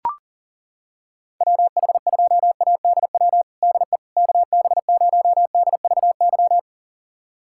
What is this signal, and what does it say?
Morse code, heard as a keyed tone, about 40 wpm, 700 Hz. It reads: WH2ADW DE KB0DVY
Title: USA Callsign DE Callsign - 6 Characters - 40wpm